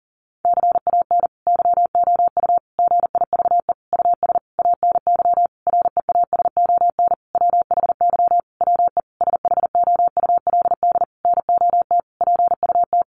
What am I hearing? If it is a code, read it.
Morse code, heard as a keyed tone, about 40 wpm, 700 Hz. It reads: CAN YOU GIVE US ANY REASON WHY WE SHOULD NOT PUT
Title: Sentences from Top 1100 Words 40wpm